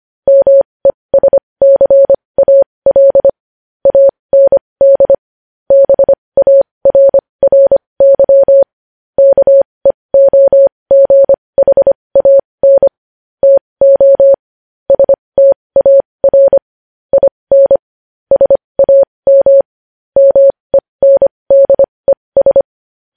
Morse code, heard as a keyed tone, about 25 wpm, spaced as for 20 wpm, 550 Hz. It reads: MESCAL AND BARRY KEOGHAN TO STAR IN SAM MENDES
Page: news-headlines-in-morse-code-at-20-wpm